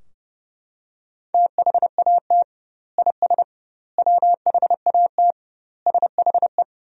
Morse code, heard as a keyed tone, about 30 wpm, 700 Hz. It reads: THAT IS WHAT SHE
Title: 2x Word-Spacing - Sentences from Top 100 Words - 30wpm